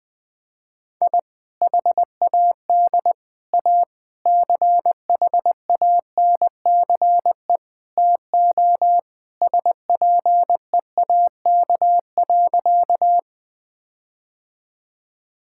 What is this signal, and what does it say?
Morse code, heard as a keyed tone, about 20 wpm, 700 Hz. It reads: I HAD A CHANCE TO SPEAK.
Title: Sentences with 4-Word Phrases - 20wpm